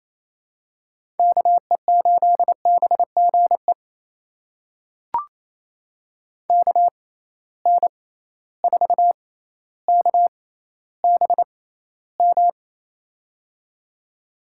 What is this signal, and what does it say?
Morse code, heard as a keyed tone, about 28 wpm, spaced as for 14 wpm, 700 Hz. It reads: KE8BGE KN4KBM
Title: USA Call Signs - 6 Characters - 6x Character Spacing - 28wpm